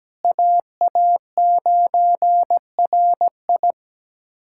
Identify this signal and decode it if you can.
Morse code, heard as a keyed tone, about 17 wpm, 700 Hz. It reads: AA9RI
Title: USA Callsign DE Callsign - 5 Characters - 17wpm